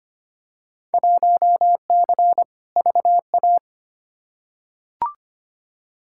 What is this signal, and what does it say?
Morse code, heard as a keyed tone, about 25 wpm, 700 Hz. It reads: 1C VA
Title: Field Day Exchanges Encore - 25wpm